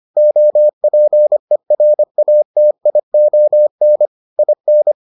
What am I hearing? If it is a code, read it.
Morse code, heard as a keyed tone, about 25 wpm, 600 Hz. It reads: OPERATION IN